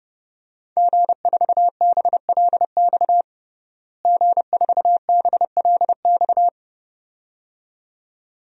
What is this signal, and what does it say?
Morse code, heard as a keyed tone, about 30 wpm, 700 Hz. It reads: G4BLX G4BLX
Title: DXCC-100 Call Signs - 5 Characters - Repeated 2x - 30wpm